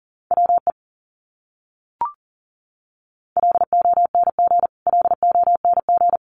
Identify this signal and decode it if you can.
Morse code, heard as a keyed tone, about 38 wpm, 700 Hz. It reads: WE LONG LONG
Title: Top 200 Words Repeated 40wpm